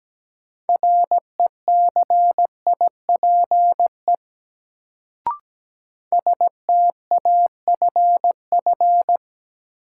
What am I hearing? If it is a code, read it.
Morse code, heard as a keyed tone, about 17 wpm, 700 Hz. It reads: RECIPE STAFF
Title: Sentences from Top 1200 Words - New Word Drill Alt - 17wpm